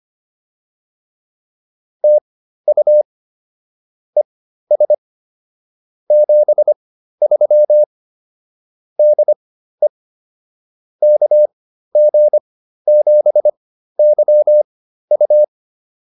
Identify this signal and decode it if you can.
Morse code, heard as a keyed tone, about 25 wpm, spaced as for 13 wpm, 600 Hz. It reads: TU ES 73 DE KG7YU